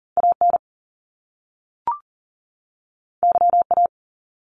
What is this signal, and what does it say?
Morse code, heard as a keyed tone, about 40 wpm, 700 Hz. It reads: AN YA
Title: Top Two Letter Combinations in Words 40wpm